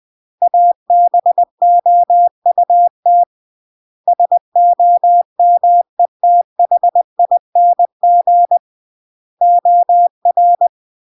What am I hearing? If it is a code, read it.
Morse code, heard as a keyed tone, about 20 wpm, 700 Hz. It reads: ABOUT SOMETHING OR
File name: CW / jodorowsky / 82 commandments 20-20